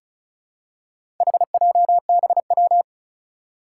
Morse code, heard as a keyed tone, about 36 wpm, 690 Hz. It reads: HJBW
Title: Four Challenging Letter-Number Combinations 35wpm